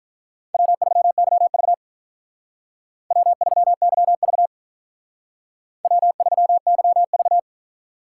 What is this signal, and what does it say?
Morse code, heard as a keyed tone, about 46 wpm, 700 Hz. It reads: W3YV W3YV W3YV